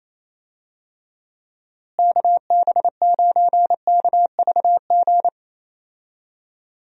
Morse code, heard as a keyed tone, about 28 wpm, 700 Hz. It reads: KB9KVG
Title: USA Call Signs - 6 Characters - Repeated 3x - 28wpm